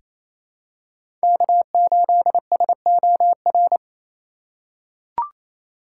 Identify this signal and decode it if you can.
Morse code, heard as a keyed tone, about 28 wpm, 700 Hz. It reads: K8SOR
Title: USA Call Signs - 5 Characters - 4x Character Spacing - 28wpm